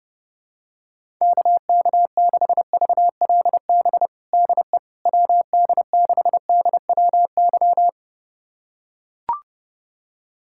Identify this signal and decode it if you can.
Morse code, heard as a keyed tone, about 30 wpm, 700 Hz. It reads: KK6VLB DE WD6DWY
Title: USA Callsign DE Callsign - 6 Characters - 30wpm